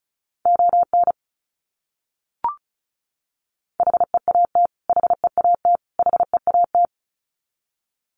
Morse code, heard as a keyed tone, about 34 wpm, 700 Hz. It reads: ON HEAT HEAT HEAT
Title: Top 500 Words Repeated 35wpm